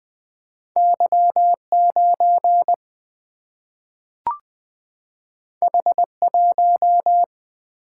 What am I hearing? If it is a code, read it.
Morse code, heard as a keyed tone, about 20 wpm, 700 Hz. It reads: Y9 H1
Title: Single-Letter Single-Number 20wpm